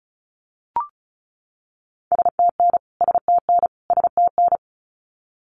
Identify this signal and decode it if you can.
Morse code, heard as a keyed tone, about 34 wpm, 700 Hz. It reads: STN STN STN
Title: Abbreviations Repeated 3x - 35wpm